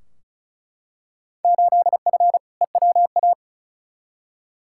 Morse code, heard as a keyed tone, about 36 wpm, 700 Hz. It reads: 8F EWA
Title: Field Day Exchanges Encore - 35wpm